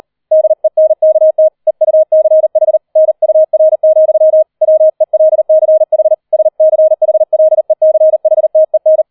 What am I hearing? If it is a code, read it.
Morse code, heard as a keyed tone, about 38 wpm, 630 Hz. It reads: DENKT EUCH NUR, WELCH SCHLECHTEN
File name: 40wpm